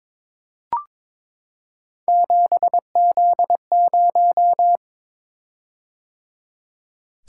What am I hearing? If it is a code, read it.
Morse code, heard as a keyed tone, about 22 wpm, 700 Hz. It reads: 7Z0
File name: practice / 22wpm / Three Letter-Number Combinations - 22wpm